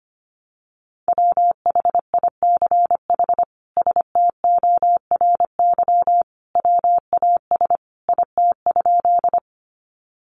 Morse code, heard as a keyed tone, about 25 wpm, 700 Hz. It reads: WHICH STORY WAS IT?
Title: Sentences from Top 300 Words 25wpm